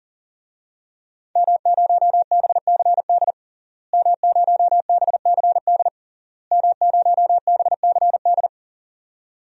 Morse code, heard as a keyed tone, about 40 wpm, 700 Hz. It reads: M0BCD M0BCD M0BCD
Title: DXCC-100 Call Signs - 5 Characters - Repeated 3x - 40wpm